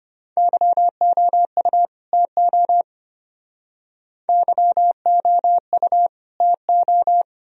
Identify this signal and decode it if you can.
Morse code, heard as a keyed tone, about 33 wpm, 700 Hz. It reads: YOU TO YOU TO